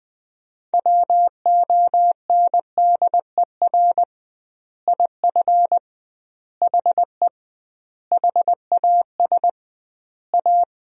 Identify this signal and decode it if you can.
Morse code, heard as a keyed tone, about 20 wpm, 700 Hz. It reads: WONDER IF HE HAS A
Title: 2x Word-Spacing - Sentences from Top 500 Words - 20wpm